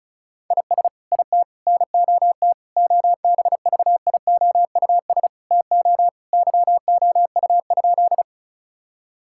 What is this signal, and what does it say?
Morse code, heard as a keyed tone, about 36 wpm, 700 Hz. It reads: IS IT NOT OBVIOUS TO YOU?
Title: Sentences from Top 1600 Words 35wpm